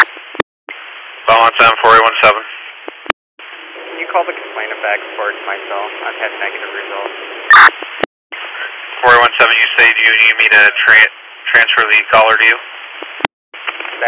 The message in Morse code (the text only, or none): none